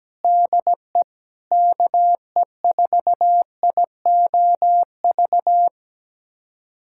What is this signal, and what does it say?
Morse code, heard as a keyed tone, about 17 wpm, 700 Hz. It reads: DE KE4IOV
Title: USA Callsign DE Callsign - 6 Characters - 17wpm